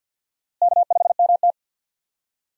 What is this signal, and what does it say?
Morse code, heard as a keyed tone, about 46 wpm, 700 Hz. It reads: K5MT
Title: USA Call Signs - 4 Characters - Repeated 2x - 50wpm